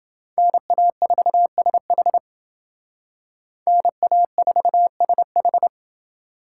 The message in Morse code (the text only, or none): NA4SH NA4SH